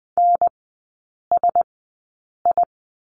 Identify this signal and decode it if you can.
Morse code, heard as a keyed tone, about 20 wpm, spaced as for 8 wpm, 700 Hz. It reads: NSI
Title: 035 Lesson 7e - Letter s - Three Characters 20wpm